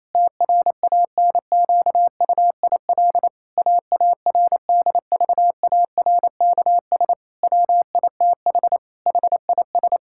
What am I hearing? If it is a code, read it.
Morse code, heard as a keyed tone, about 28 wpm, 700 Hz. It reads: TRANQUIL AARDVARKS WITH HIS
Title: Pangram Challenge v2 - 28wpm